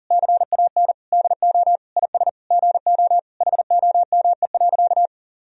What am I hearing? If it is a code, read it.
Morse code, heard as a keyed tone, about 40 wpm, 700 Hz. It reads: CAN DO IS GO HOME.
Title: Sentences from Top 200 Words and Abbreviations 40wpm